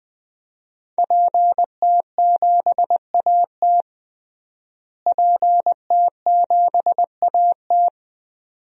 Morse code, heard as a keyed tone, about 19 wpm, 700 Hz. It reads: PT7AT PT7AT
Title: DXCC-100 Call Signs - 5 Characters - Repeated 2x - 20wpm